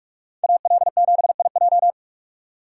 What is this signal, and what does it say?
Morse code, heard as a keyed tone, about 44 wpm, 700 Hz. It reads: AL7IJ